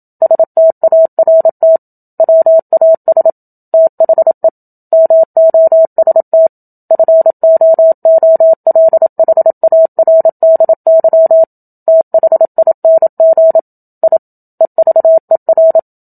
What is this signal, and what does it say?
Morse code, heard as a keyed tone, about 27 wpm, 650 Hz. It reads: START WAS THE MOST FOOLHARDY THING I EVER
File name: Book Two / 01-Chapter 1 - 27 WPM